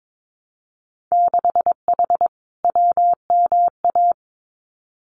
Morse code, heard as a keyed tone, about 22 wpm, 700 Hz. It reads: 6H WMA
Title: Winter Field Day Exchanges Encore 22wpm